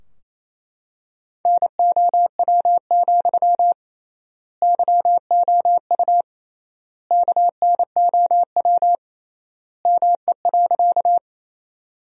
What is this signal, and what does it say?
Morse code, heard as a keyed tone, about 27 wpm, 700 Hz. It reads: NOW, YOU KNOW ME.
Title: 3x Word-Spacing - Sentences from Top 200 Words - 28wpm